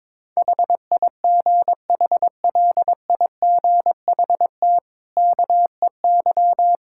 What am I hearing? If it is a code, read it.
Morse code, heard as a keyed tone, about 21 wpm, 690 Hz. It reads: HIGHLIGHT KEY
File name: practice / 22wpm / Easy Sentences with Hints - 22wpm